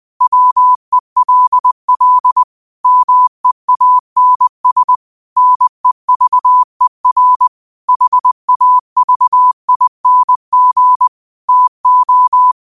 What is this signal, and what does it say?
Morse code, heard as a keyed tone, about 20 wpm, 1000 Hz. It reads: WELL MEANS NEVER HAVING TO
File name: qotdfiles / 20wpm